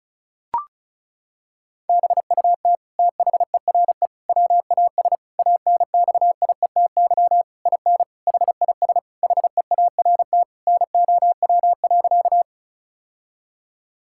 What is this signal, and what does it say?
Morse code, heard as a keyed tone, about 34 wpm, 700 Hz. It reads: BUT THERE WAS ANXIETY IN HIS HEART NOW.
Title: Sentences from Top 1600 Words 35wpm